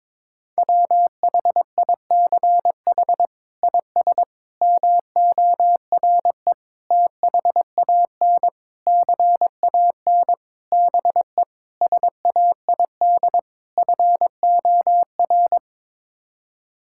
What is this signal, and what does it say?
Morse code, heard as a keyed tone, about 22 wpm, 700 Hz. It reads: WHICH IS MORE THAN CAN BE SAID FOR
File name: practice / 22wpm / Sets of 8 Words - 22wpm